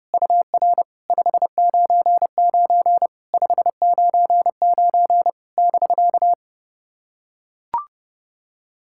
Morse code, heard as a keyed tone, about 30 wpm, 700 Hz. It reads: UR 599 599 <BK>
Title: SOTA QSO Practice 30wpm